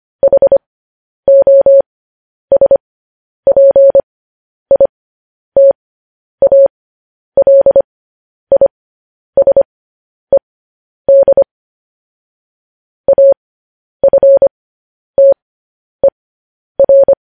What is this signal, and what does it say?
Morse code, heard as a keyed tone, about 25 wpm, spaced as for 10 wpm, 550 Hz. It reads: HOSPITALISED AFTER
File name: podcast / News headlines-1x-25x10